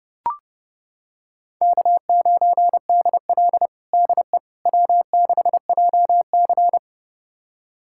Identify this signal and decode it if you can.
Morse code, heard as a keyed tone, about 30 wpm, 700 Hz. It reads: K9DL DE W6JC
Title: USA Callsign DE Callsign - 4 Characters - 30wpm